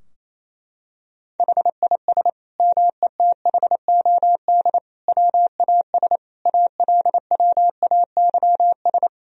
Morse code, heard as a keyed tone, about 28 wpm, 700 Hz. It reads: HIS METHOD WAS ALWAYS
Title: Sentences from Top 1000 Words - Review Part 3 - 28wpm